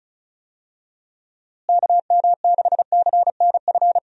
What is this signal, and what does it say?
Morse code, heard as a keyed tone, about 34 wpm, 700 Hz. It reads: KM6CNF
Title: USA Call Signs - 6 Characters - Repeated 2x - 35wpm